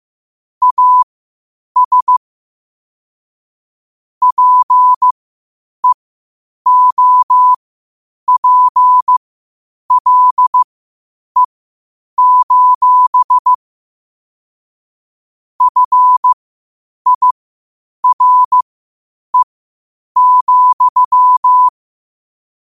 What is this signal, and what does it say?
Morse code, heard as a keyed tone, about 15 wpm, spaced as for 8 wpm, 1000 Hz. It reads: AS PEOPLE: FIRE,
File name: 8wpm